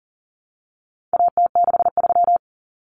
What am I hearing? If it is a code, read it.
Morse code, heard as a keyed tone, about 40 wpm, 700 Hz. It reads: AT63
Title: Common Cut Numbers - 4-Digit 40wpm